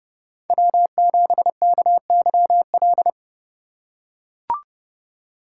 Morse code, heard as a keyed tone, about 29 wpm, 700 Hz. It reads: W7KYL